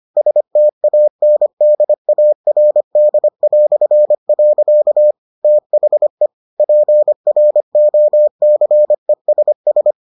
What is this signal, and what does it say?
Morse code, heard as a keyed tone, about 25 wpm, 600 Hz. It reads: STANDARD". THE PROCESS